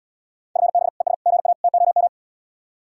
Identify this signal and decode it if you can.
Morse code, heard as a keyed tone, about 80 wpm, 700 Hz. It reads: FB ES GA TOM